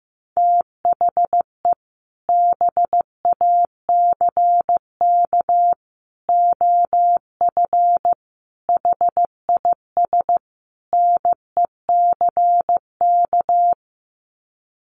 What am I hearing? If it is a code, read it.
Morse code, heard as a keyed tone, about 15 wpm, 700 Hz. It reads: THE BACK OF HIS NECK
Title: Sets of 5 Words Encore 15wpm